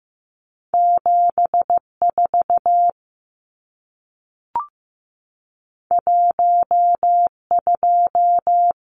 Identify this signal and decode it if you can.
Morse code, heard as a keyed tone, about 15 wpm, 700 Hz. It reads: 74 12
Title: Numbers - 2-Digit 15wpm